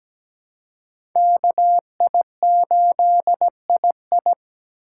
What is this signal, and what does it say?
Morse code, heard as a keyed tone, about 17 wpm, 700 Hz. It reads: KI8II
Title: USA Call Signs - 5 Characters - 4x Character Spacing - 17wpm